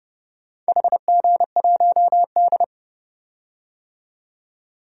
Morse code, heard as a keyed tone, about 32 wpm, 700 Hz. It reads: HG1D
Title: DXCC-100 Call Signs - 4 Characters - 30wpm